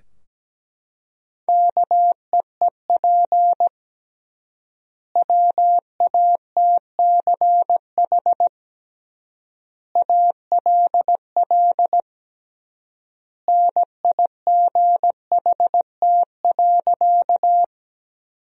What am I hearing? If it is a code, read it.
Morse code, heard as a keyed tone, about 17 wpm, 700 Hz. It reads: KEEP WATCH ALL NIGHT.
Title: 3x Word-Spacing - Sentences from Top 300 Words - 17wpm